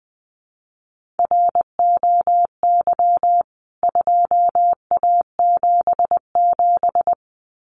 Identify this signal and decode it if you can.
Morse code, heard as a keyed tone, about 20 wpm, 700 Hz. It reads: ROY 2A77
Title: CWT Exchange 20wpm